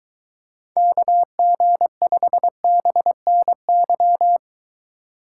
Morse code, heard as a keyed tone, about 23 wpm, 700 Hz. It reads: KG5BNY